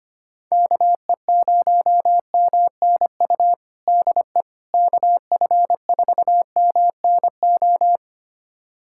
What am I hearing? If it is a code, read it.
Morse code, heard as a keyed tone, about 25 wpm, 700 Hz. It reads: KE0MNU DE KF4MNO
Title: USA Callsign DE Callsign - 6 Characters - 25wpm